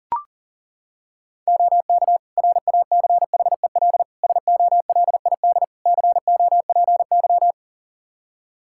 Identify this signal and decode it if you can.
Morse code, heard as a keyed tone, about 40 wpm, 700 Hz. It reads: OK RACHEL SOLID COPY
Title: K1USN SST QSO Practice 40wpm